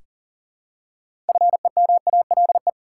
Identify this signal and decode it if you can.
Morse code, heard as a keyed tone, about 40 wpm, 700 Hz. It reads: FEMALE
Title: Six Letter Words - 40wpm